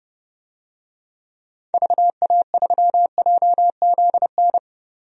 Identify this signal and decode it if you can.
Morse code, heard as a keyed tone, about 31 wpm, 700 Hz. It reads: VA3JZN
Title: DXCC-100 Call Signs - 6 Characters - 30wpm